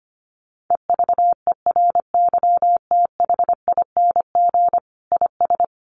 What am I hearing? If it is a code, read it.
Morse code, heard as a keyed tone, about 25 wpm, 700 Hz. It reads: EVERYTHING IS
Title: Sentences from Top 700 Words 25wpm